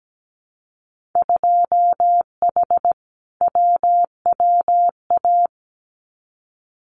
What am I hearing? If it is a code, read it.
Morse code, heard as a keyed tone, about 17 wpm, 700 Hz. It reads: 2H WWA
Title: Winter Field Day Exchanges 17wpm